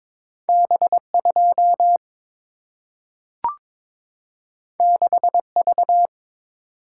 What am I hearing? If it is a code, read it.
Morse code, heard as a keyed tone, about 22 wpm, 700 Hz. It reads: B 2 6 V
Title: Single-Letter Single-Number 22wpm